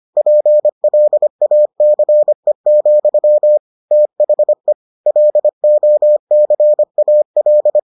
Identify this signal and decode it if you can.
Morse code, heard as a keyed tone, about 25 wpm, 600 Hz. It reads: PLACE, THE LOCAL